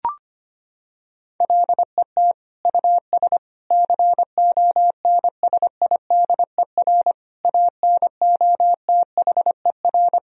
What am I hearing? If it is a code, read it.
Morse code, heard as a keyed tone, about 25 wpm, 700 Hz. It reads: LET US CONSIDER ANOTHER
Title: Sentences from Top 1000 Words - Review Part 4 - 25wpm